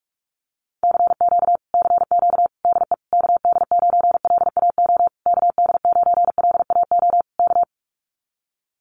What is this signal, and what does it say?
Morse code, heard as a keyed tone, about 44 wpm, 700 Hz. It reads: CQ CQ DE KD9LAO KD9LAO K
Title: Common QSO Phrases Encore 45wpm